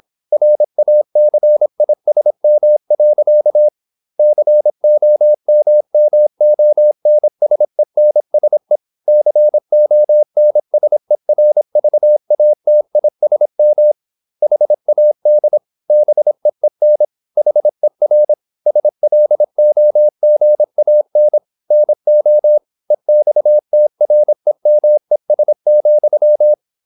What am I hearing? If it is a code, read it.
Morse code, heard as a keyed tone, about 26 wpm, 600 Hz. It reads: RACISM. COMMONSENSE CONSERVATISM HAD BEEN HER SLOGAN NO EXTREMES,
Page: Never 26wpm | CW med Gnister